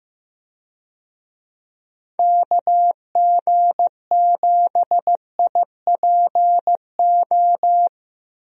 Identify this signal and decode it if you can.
Morse code, heard as a keyed tone, about 15 wpm, 700 Hz. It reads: KG7IPO
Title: USA Call Signs - 6 Characters - Repeated 2x - 15wpm